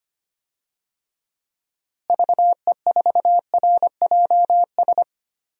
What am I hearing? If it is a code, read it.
Morse code, heard as a keyed tone, about 25 wpm, 700 Hz. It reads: VE4RJS
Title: DXCC-100 Call Signs - 6 Characters - Repeated 3x - 25wpm